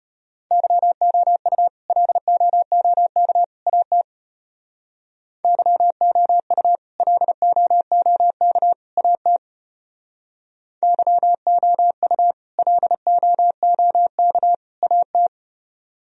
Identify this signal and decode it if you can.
Morse code, heard as a keyed tone, about 32 wpm, 700 Hz. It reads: YOU LOOK AT YOU LOOK AT YOU LOOK AT